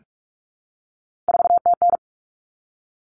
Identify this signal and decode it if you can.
Morse code, heard as a keyed tone, about 44 wpm, 700 Hz. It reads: 4TN
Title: Common Cut Numbers - 3-Digit 45wpm